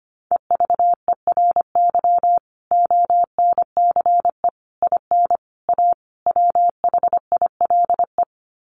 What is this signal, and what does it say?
Morse code, heard as a keyed tone, about 25 wpm, 700 Hz. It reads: EVERY ONCE IN A WHILE
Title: Sets of 5 Words 25wpm